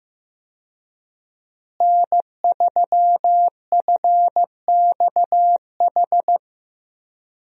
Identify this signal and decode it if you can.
Morse code, heard as a keyed tone, about 14 wpm, 700 Hz. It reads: N3FXH
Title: USA Call Signs - 5 Characters - Repeated 2x - 15wpm